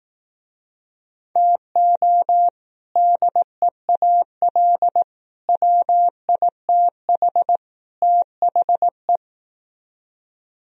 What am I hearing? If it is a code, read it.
Morse code, heard as a keyed tone, about 18 wpm, 700 Hz. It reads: TO DEAL WITH THE
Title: Speed-Racing - Sets of 4 Words 15wpm